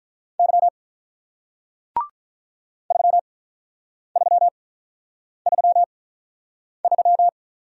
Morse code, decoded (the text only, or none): Y 3333